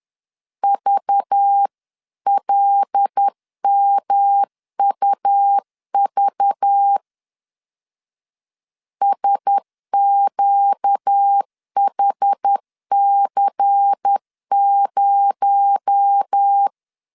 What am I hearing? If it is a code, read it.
Morse code, heard as a keyed tone, about 10 wpm, 800 Hz. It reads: VLMUV SQHC0